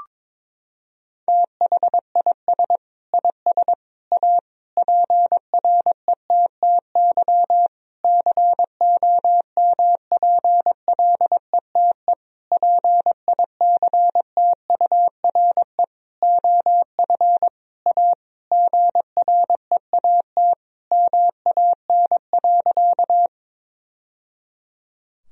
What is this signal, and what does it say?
Morse code, heard as a keyed tone, about 22 wpm, 700 Hz. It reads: THIS IS A PRETTY COMPLETE PICTURE OF A GREAT MAN.
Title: Sentences from Top 1000 Words - Review Part 2 - 22wpm